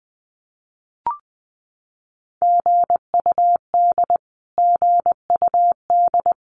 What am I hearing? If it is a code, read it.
Morse code, heard as a keyed tone, about 20 wpm, 700 Hz. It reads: GUD GUD
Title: Abbreviations Repeated 3x - 20wpm